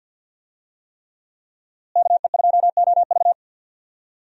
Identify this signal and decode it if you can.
Morse code, heard as a keyed tone, about 50 wpm, 700 Hz. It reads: KE2OV
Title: USA Call Signs - 5 Characters - Repeated 3x - 50wpm